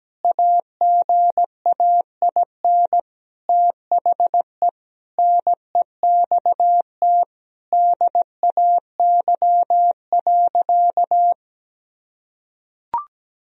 Morse code, heard as a keyed tone, about 17 wpm, 700 Hz. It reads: AGAIN THE NEXT DAY.
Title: Sentences from Top 1000 Words - Review Part 1 - 17wpm